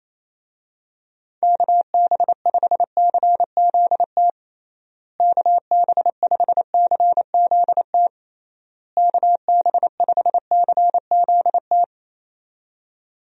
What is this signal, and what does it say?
Morse code, heard as a keyed tone, about 27 wpm, 700 Hz. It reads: KB5CZT KB5CZT KB5CZT
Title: USA Call Signs - 6 Characters - Repeated 3x - 28wpm